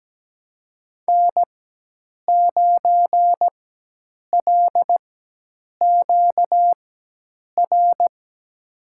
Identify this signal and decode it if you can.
Morse code, heard as a keyed tone, about 17 wpm, spaced as for 8 wpm, 700 Hz. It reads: N9LQR
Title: USA Call Signs - 5 Characters - 4x Character Spacing - 17wpm